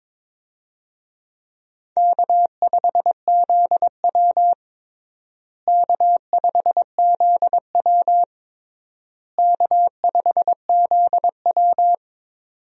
Morse code, heard as a keyed tone, about 22 wpm, 700 Hz. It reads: K5ZW K5ZW K5ZW
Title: USA Call Signs - 4 Characters - Repeated 3x - 22wpm